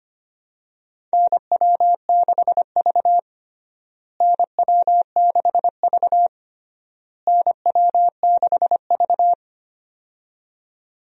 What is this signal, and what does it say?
Morse code, heard as a keyed tone, about 25 wpm, 700 Hz. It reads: NW6V NW6V NW6V
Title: USA Call Signs - 4 Characters - Repeated 3x - 25wpm